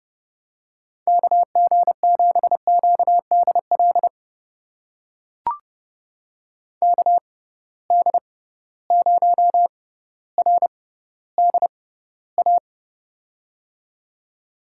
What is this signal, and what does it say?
Morse code, heard as a keyed tone, about 32 wpm, spaced as for 16 wpm, 700 Hz. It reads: KG7QDL KD0RDA
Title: USA Call Signs - 6 Characters - 6x Character Spacing - 30wpm